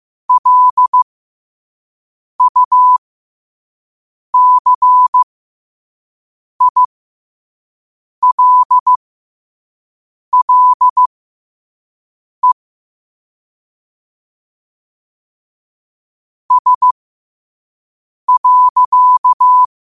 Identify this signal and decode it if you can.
Morse code, heard as a keyed tone, about 15 wpm, spaced as for 5 wpm, 1000 Hz. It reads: LUCILLE S.
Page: Quotes for Wed, 13 Aug 2025 in Morse Code at 5 words per minute.